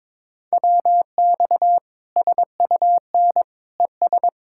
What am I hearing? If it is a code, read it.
Morse code, heard as a keyed tone, about 22 wpm, 700 Hz. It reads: WX SUN ES
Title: K1USN SST QSO Practice 22wpm